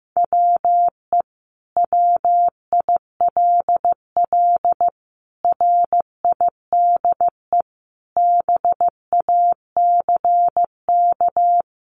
Morse code, heard as a keyed tone, about 15 wpm, 700 Hz. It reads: WE WILL RIDE BACK
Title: Sentences from Top 1200 Words 15wpm